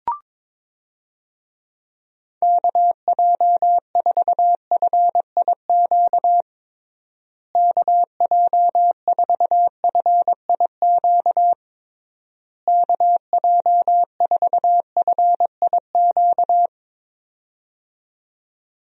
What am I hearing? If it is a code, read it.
Morse code, heard as a keyed tone, about 22 wpm, 700 Hz. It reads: KJ4FIQ KJ4FIQ KJ4FIQ
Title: USA Call Signs - 6 Characters - Repeated 3x - 22wpm